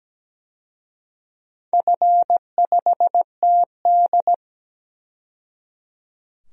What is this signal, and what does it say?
Morse code, heard as a keyed tone, about 17 wpm, 700 Hz. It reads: F5TD
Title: DXCC-100 Call Signs - 4 Characters - 17wpm